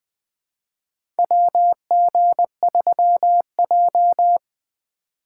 Morse code, heard as a keyed tone, about 20 wpm, 700 Hz. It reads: WG3J
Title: USA Call Signs - 4 Characters - 6x Character Spacing - 20wpm